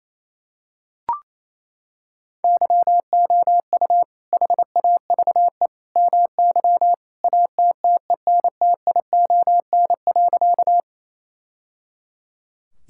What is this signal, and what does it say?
Morse code, heard as a keyed tone, about 27 wpm, 700 Hz. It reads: YOU HAVE MY ATTENTION.
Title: Sentences from Top 1000 Words - Review Part 6 - 28wpm